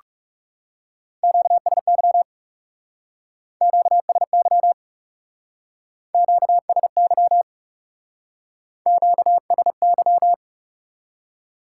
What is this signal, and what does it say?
Morse code, heard as a keyed tone, about 40 wpm, 700 Hz. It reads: QSY QSY QSY QSY